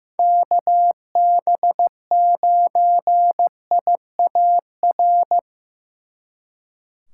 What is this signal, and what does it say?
Morse code, heard as a keyed tone, about 18 wpm, 700 Hz. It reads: KB9IAR
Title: Speed Racing - USA Call Signs 15wpm